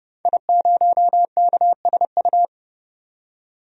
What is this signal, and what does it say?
Morse code, heard as a keyed tone, about 30 wpm, 690 Hz. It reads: I0KSU
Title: DXCC-100 Call Signs - 5 Characters - 30wpm